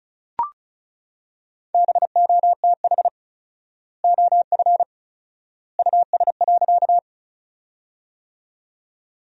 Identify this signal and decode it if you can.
Morse code, heard as a keyed tone, about 34 wpm, 700 Hz. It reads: BOTH OF US.
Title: 4x Word-Spacing - Sentences from Top 300 Words - 35wpm